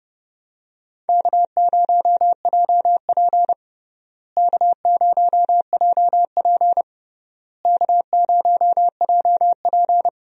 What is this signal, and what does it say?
Morse code, heard as a keyed tone, about 30 wpm, 700 Hz. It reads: K0JP K0JP K0JP
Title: USA Call Signs - 4 Characters - Repeated 3x - 30wpm